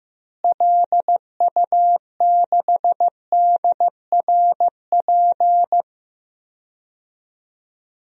Morse code, heard as a keyed tone, about 15 wpm, 700 Hz. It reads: LU6DRP
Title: DXCC-100 Call Signs - 6 Characters - Repeated 3x - 15wpm